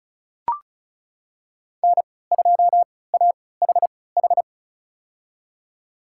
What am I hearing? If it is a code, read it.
Morse code, heard as a keyed tone, about 34 wpm, spaced as for 20 wpm, 700 Hz. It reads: N2AHH
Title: USA Call Signs - 5 Characters - 3x Character Spacing - 35wpm